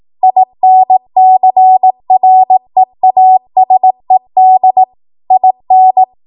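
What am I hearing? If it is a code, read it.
Morse code, heard as a keyed tone, about 18 wpm, 750 Hz. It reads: INCREASED IN
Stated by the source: ARRL 18 wpm Morse code practice archive transmission.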